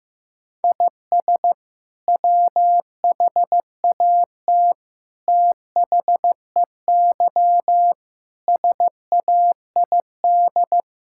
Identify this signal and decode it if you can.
Morse code, heard as a keyed tone, about 15 wpm, 700 Hz. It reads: IS WHAT THEY SAID
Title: Sentences from Top 1000 Words - Review Part 5 - 15wpm